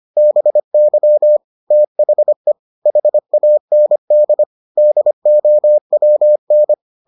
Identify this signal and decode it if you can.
Morse code, heard as a keyed tone, about 25 wpm, 600 Hz. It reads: BY THE HAND DOWN